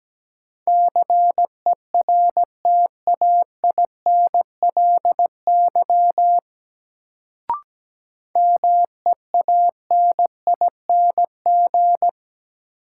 Morse code, heard as a keyed tone, about 17 wpm, 700 Hz. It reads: CERTAINLY MEANING
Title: Sentences from Top 900 Words - New Word Drill - 17wpm